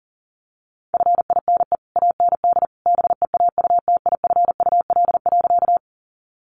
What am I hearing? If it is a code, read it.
Morse code, heard as a keyed tone, about 38 wpm, 700 Hz. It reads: FINE AND BEAUTIFUL.
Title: Sentences from Top 900 Words 40wpm